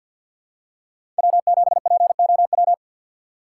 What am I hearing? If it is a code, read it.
Morse code, heard as a keyed tone, about 49 wpm, 700 Hz. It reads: W7POW
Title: USA Call Signs - 4x Character Spacing - 50wpm